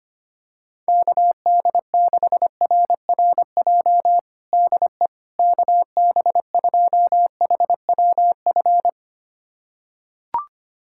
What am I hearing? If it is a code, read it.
Morse code, heard as a keyed tone, about 25 wpm, 700 Hz. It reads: KD6RRJ DE KB2HWF
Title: USA Callsign DE Callsign - 6 Characters - 25wpm